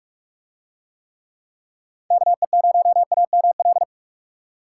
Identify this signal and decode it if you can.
Morse code, heard as a keyed tone, about 41 wpm, 700 Hz. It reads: KE0AML